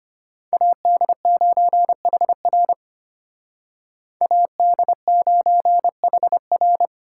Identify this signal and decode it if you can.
Morse code, heard as a keyed tone, about 33 wpm, 700 Hz. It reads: AD9HR AD9HR